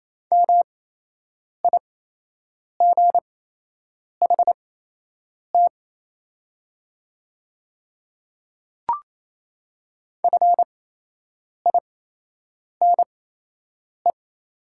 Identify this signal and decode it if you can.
Morse code, heard as a keyed tone, about 28 wpm, spaced as for 8 wpm, 700 Hz. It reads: MIGHT FINE
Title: 8x Character Spacing - Top 500 Words - 28wpm